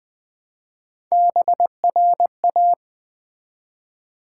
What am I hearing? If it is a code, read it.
Morse code, heard as a keyed tone, about 20 wpm, 700 Hz. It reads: BRA
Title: Build A Word - 1 Letter At A Time - 5-Letter Words - 20wpm